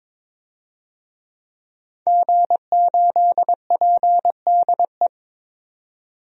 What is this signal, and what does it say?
Morse code, heard as a keyed tone, about 22 wpm, 700 Hz. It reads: G8PDE
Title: DXCC-100 Call Signs - 5 Characters - 22wpm